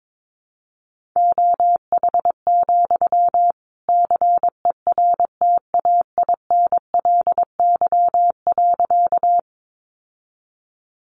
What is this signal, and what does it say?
Morse code, heard as a keyed tone, about 22 wpm, 700 Hz. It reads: OH, CERTAINLY.
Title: Sentences from Top 900 Words 22wpm